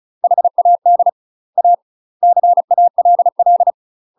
Morse code, heard as a keyed tone, about 36 wpm, 700 Hz. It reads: HAD A CALL
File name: UK MASTER QSOs TRAINING / 35WPM / UK MASTER QSOs 0016